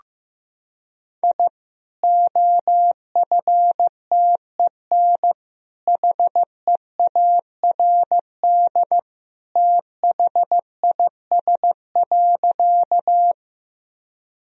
Morse code, heard as a keyed tone, about 15 wpm, 700 Hz. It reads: I OFTEN HEARD THIS.
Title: Sentences from Top 1000 Words - Review Part 1 - 15wpm